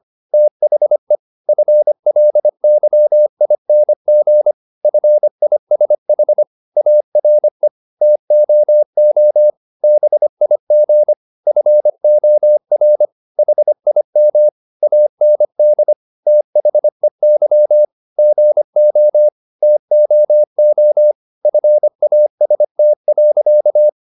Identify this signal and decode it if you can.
Morse code, heard as a keyed tone, about 25 wpm, 600 Hz. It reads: THE FLYING FISH ARE TOO BIG FOR HIM AND THEY GO TOO FAST.
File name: the old man and the sea part 1 25wpm